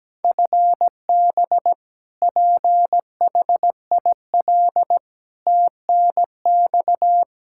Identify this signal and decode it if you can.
Morse code, heard as a keyed tone, about 17 wpm, 700 Hz. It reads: FB PHIL TNX
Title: Practice QSO - Exchange 3 - More Info - 17wpm